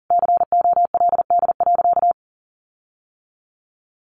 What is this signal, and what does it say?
Morse code, heard as a keyed tone, about 40 wpm, 700 Hz. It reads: COLD.
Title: Sentences from Top 1500 Words 40wpm